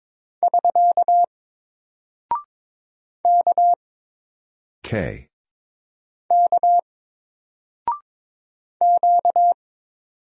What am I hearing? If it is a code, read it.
Morse code, heard as a keyed tone, about 22 wpm, 700 Hz. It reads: <SK> K K Q
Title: Single Character (Letter:Number:Punctuation:Prosign) 22wpm